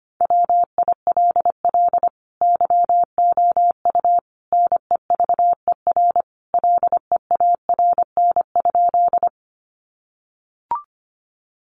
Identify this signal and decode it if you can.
Morse code, heard as a keyed tone, about 25 wpm, 700 Hz. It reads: WILL YOU NEVER LEARN?
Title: Sentences from Top 300 Words 25wpm